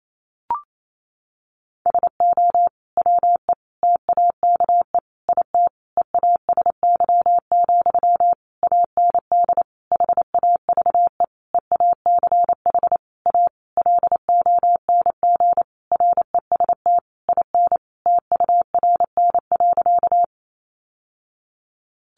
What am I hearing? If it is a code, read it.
Morse code, heard as a keyed tone, about 28 wpm, 700 Hz. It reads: SO WE TAKE IT EASY, AND HAVE EACH A LONG REST IN TURN.
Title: Sentences from Top 700 Words 28wpm